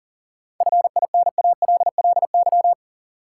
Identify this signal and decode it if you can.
Morse code, heard as a keyed tone, about 40 wpm, 700 Hz. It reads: FINALLY
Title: Sentences from Top 1000 Words - New Word Drill - 40wpm